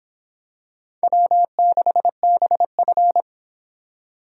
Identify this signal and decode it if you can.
Morse code, heard as a keyed tone, about 25 wpm, 700 Hz. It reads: W6BF